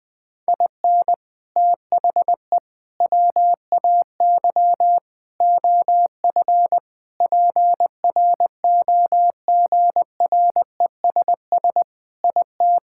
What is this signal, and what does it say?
Morse code, heard as a keyed tone, about 20 wpm, 700 Hz. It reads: IN THE WAY OF PROGRESS IT
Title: Sentences with 4-Word Phrases - 20wpm